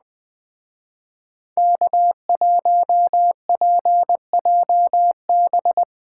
Morse code, heard as a keyed tone, about 21 wpm, 700 Hz. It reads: K1PJB